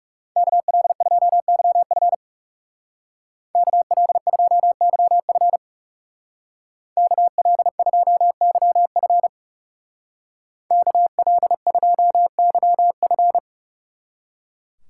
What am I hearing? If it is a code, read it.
Morse code, heard as a keyed tone, about 36 wpm, 700 Hz. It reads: KL2YF KL2YF KL2YF KL2YF
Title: Speed Racing - USA 2x2 Call Signs 30wpm